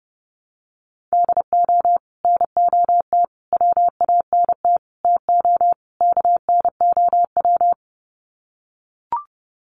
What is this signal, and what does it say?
Morse code, heard as a keyed tone, about 30 wpm, 700 Hz. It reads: DO NOT WANT TO KNOW
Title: Sets of 5 Words Encore 30wpm